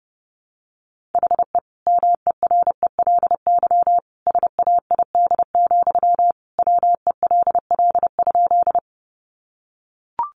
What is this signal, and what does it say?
Morse code, heard as a keyed tone, about 30 wpm, 700 Hz. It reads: HE MERELY SAID, WELL?
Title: Sentences from Top 1400 Words 30wpm